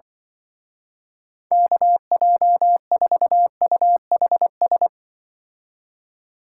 Morse code, heard as a keyed tone, about 24 wpm, 700 Hz. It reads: KJ4UHS
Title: Speed Racing - USA 2x3 Call Signs 20wpm